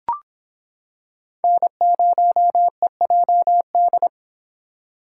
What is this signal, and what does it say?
Morse code, heard as a keyed tone, about 26 wpm, 700 Hz. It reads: N0EJD